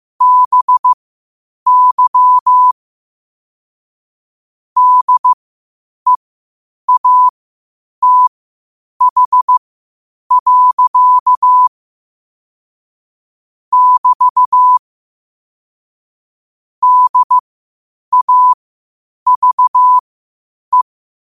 Morse code, heard as a keyed tone, about 15 wpm, spaced as for 8 wpm, 1000 Hz. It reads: BY DEATH. <BT> DAVE
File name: qotdfiles / 8wpm